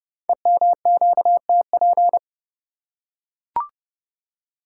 Morse code, heard as a keyed tone, about 30 wpm, 700 Hz. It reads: EMQTP
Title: Code Groups - 5 Letters - 30wpm